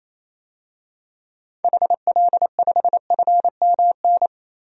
Morse code, heard as a keyed tone, about 28 wpm, 700 Hz. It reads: HL5FMN
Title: DXCC-100 Call Signs - 6 Characters - 28wpm